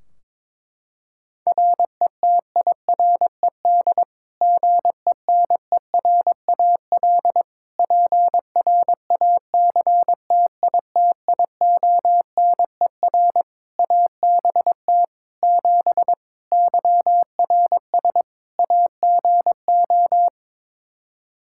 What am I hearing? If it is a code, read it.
Morse code, heard as a keyed tone, about 21 wpm, 700 Hz. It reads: RETIRED GENERAL PRACTITIONER ABT 7 YRS AGO
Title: Practice QSO - Exchange 3 - More Info - 22wpm